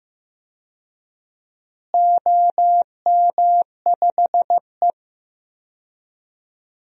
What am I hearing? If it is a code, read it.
Morse code, heard as a keyed tone, about 15 wpm, 700 Hz. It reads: OM5E
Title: DXCC-100 Call Signs - 4 Characters - Repeated 2x - 15wpm